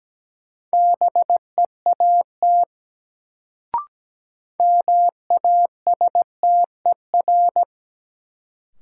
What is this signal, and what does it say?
Morse code, heard as a keyed tone, about 17 wpm, 700 Hz. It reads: BEAT MASTER
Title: Sentences from Top 1300 Words - New Word Drill Alt - 17wpm